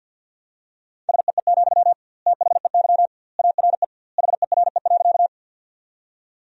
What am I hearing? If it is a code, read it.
Morse code, heard as a keyed tone, about 50 wpm, 700 Hz. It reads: SEE, THEY ARE HERE.
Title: 2x Word-Spacing - Sentences from Top 200 Words - 50wpm